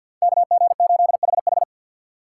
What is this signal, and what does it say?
Morse code, heard as a keyed tone, about 46 wpm, 700 Hz. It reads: KG8HH
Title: USA Call Signs - 5 Characters - 2x Character Spacing - 50wpm